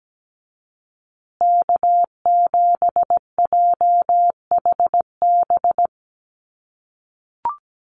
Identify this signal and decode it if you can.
Morse code, heard as a keyed tone, about 17 wpm, 700 Hz. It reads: K7JHB
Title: USA Call Signs Repeated 2x 17wpm